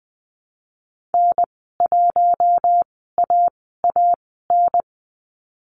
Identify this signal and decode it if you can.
Morse code, heard as a keyed tone, about 20 wpm, spaced as for 14 wpm, 700 Hz. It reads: N1AAN
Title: 042 Lesson 8g - Number 1 - Call Signs 20wpm